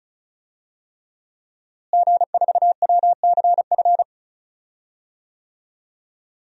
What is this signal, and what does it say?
Morse code, heard as a keyed tone, about 34 wpm, 690 Hz. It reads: G4WCF
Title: DXCC-100 Call Signs - 5 Characters - Repeated 3x - 35wpm